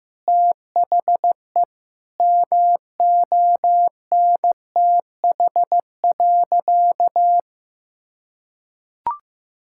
Morse code, heard as a keyed tone, about 16 wpm, 700 Hz. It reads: THE MONTH.
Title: Sentences from Top 1000 Words - Review Part 5 - 15wpm